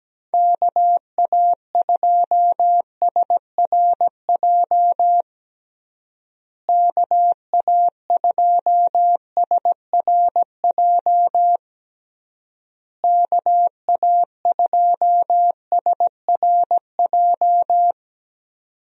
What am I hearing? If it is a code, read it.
Morse code, heard as a keyed tone, about 17 wpm, 700 Hz. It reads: KA2SRJ KA2SRJ KA2SRJ
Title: USA Call Signs - 6 Characters - Repeated 3x - 17wpm